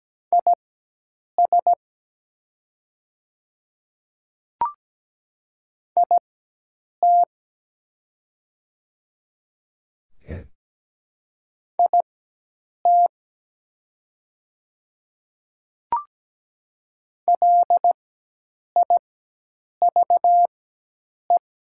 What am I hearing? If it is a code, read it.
Morse code, heard as a keyed tone, about 17 wpm, spaced as for 8 wpm, 700 Hz. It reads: IS IT IT LIVE
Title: 4x Character Spacing - Top 200 Words - 17wpm